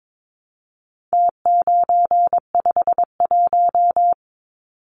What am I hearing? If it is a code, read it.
Morse code, heard as a keyed tone, about 22 wpm, 700 Hz. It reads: T951
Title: Numbers - Contest Style - 4-Digit 22wpm